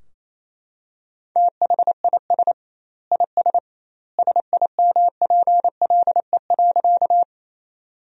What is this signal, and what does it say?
Morse code, heard as a keyed tone, about 28 wpm, 700 Hz. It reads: THIS IS SIMPLE.
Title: 2x Word-Spacing - Sentences from Top 400 Words - 28wpm